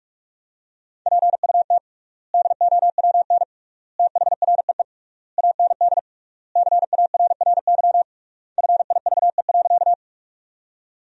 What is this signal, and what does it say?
Morse code, heard as a keyed tone, about 45 wpm, 700 Hz. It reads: PUT DOWN THREE AND CARRY FIVE.
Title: 3x Word-Spacing - Sentences from Top 400 Words - 45wpm